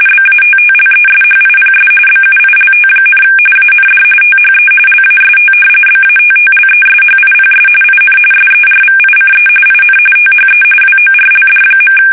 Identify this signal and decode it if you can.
RTTY, about 45.45 baud. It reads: BATTLESHIPS BUILT, AND THE LAST BATTLESHIPS IN SERVICE IN THE WORLD.
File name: fsk850 / battleshipnj